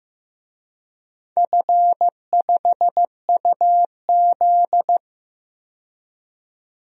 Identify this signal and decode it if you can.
Morse code, heard as a keyed tone, about 15 wpm, 700 Hz. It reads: F5UZ
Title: DXCC-100 Call Signs - 4 Characters - 15wpm